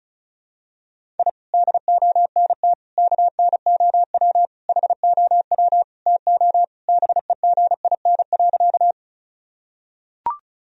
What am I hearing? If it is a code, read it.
Morse code, heard as a keyed tone, about 36 wpm, 700 Hz. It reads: I DONT KNOW HOW TO BEGIN.
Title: Sentences from Top 1000 Words - Review Part 1 - 35wpm